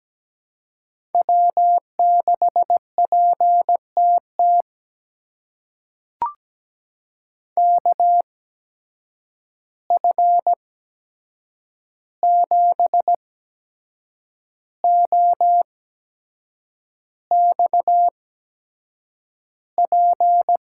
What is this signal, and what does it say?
Morse code, heard as a keyed tone, about 17 wpm, spaced as for 7 wpm, 700 Hz. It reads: W6PTT KF7OXP
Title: USA Call Signs - 8x Character Spacing - 17wpm